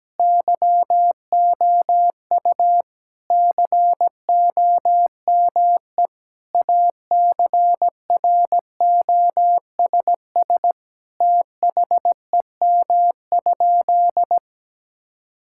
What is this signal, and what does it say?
Morse code, heard as a keyed tone, about 17 wpm, 700 Hz. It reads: YOU COME ACROSS THEM?
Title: Sentences from Top 1000 Words - Review Part 5 - 17wpm